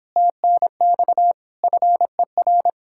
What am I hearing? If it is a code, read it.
Morse code, heard as a keyed tone, about 29 wpm, 700 Hz. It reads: TNX FER